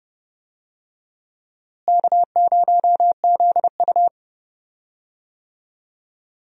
Morse code, heard as a keyed tone, about 31 wpm, 700 Hz. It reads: K0ZU
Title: USA Call Signs - 4 Characters - Repeated 2x - 30wpm